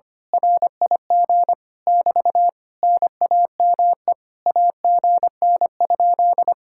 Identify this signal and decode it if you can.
Morse code, heard as a keyed tone, about 24 wpm, 700 Hz. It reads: RIG <BT> NAME AGN?
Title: Speed-Racing Common QSO Phrases - 25wpm